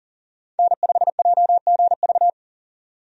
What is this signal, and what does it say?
Morse code, heard as a keyed tone, about 40 wpm, 700 Hz. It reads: N5JGV
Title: USA Call Signs - 5 Characters - 40wpm